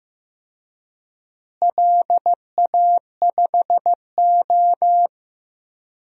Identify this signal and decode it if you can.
Morse code, heard as a keyed tone, about 15 wpm, 700 Hz. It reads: LA5O
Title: DXCC-100 Call Signs - 4 Characters - Repeated 3x - 15wpm